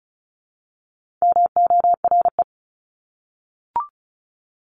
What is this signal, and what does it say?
Morse code, heard as a keyed tone, about 34 wpm, 700 Hz. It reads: MORE
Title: Top 300 Words 35wpm